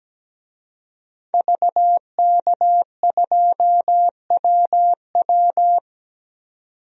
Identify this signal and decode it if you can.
Morse code, heard as a keyed tone, about 17 wpm, 700 Hz. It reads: VK2WW
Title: DXCC-100 Call Signs - 5 Characters - 17wpm